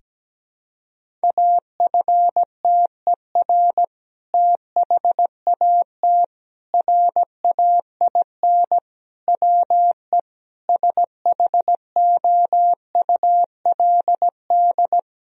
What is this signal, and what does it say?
Morse code, heard as a keyed tone, about 17 wpm, 700 Hz. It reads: AFTER THAT RAIN WE SHOULD
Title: Sentences from Top 1000 Words - Review Part 1 - 17wpm